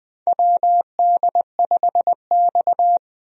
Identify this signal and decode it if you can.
Morse code, heard as a keyed tone, about 21 wpm, 700 Hz. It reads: WD5X
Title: USA Call Signs - 4 Characters - 20wpm